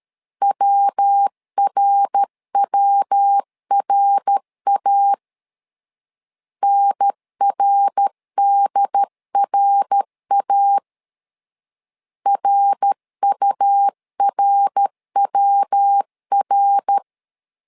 Morse code, heard as a keyed tone, about 13 wpm, 800 Hz. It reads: WRWRA NRDRA RURWR
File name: LEZIONI / 3 MWRO / 3 MWR 12wpm 1711